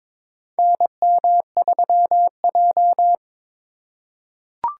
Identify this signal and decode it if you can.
Morse code, heard as a keyed tone, about 22 wpm, 700 Hz. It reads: NM3J
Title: USA Call Signs - 4 Characters - 3x Character Spacing - 22wpm